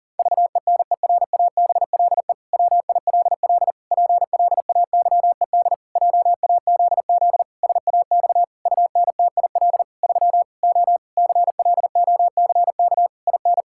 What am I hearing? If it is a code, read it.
Morse code, heard as a keyed tone, about 40 wpm, 700 Hz. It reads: VENERABLE WILL PLAYED JAZZ SAX UNTIL 3 O CLOCK IN
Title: Pangram Challenge v2 - 40wpm